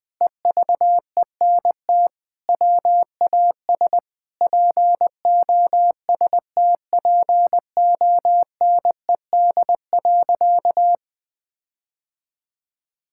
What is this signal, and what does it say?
Morse code, heard as a keyed tone, about 21 wpm, 700 Hz. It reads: EVENT WAS POSTPONED.
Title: Sentences with 8-Word Phrases - Spoken Beginning - 20wpm